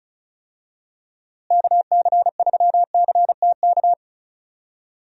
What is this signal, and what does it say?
Morse code, heard as a keyed tone, about 36 wpm, 700 Hz. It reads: KC3CTK
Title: USA Call Signs - 6 Characters - Repeated 3x - 35wpm